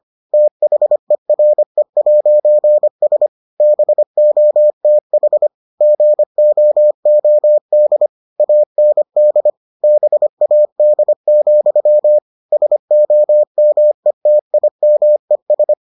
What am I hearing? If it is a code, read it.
Morse code, heard as a keyed tone, about 25 wpm, 600 Hz. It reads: THERE'S BOTH GOOD AND BAD, SOMETIMES